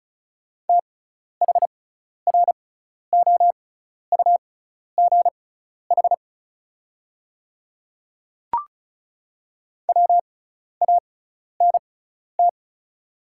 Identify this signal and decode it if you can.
Morse code, heard as a keyed tone, about 34 wpm, spaced as for 12 wpm, 690 Hz. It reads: THROUGH WANT
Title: 6x Character Spacing - Top 300 Words - 35wpm